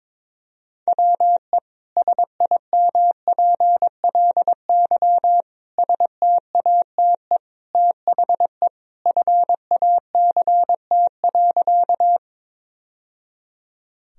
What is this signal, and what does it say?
Morse code, heard as a keyed tone, about 22 wpm, 700 Hz. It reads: WE SIMPLY STATE THE FACT.
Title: Sentences from Top 1000 Words - Review Part 4 - 22wpm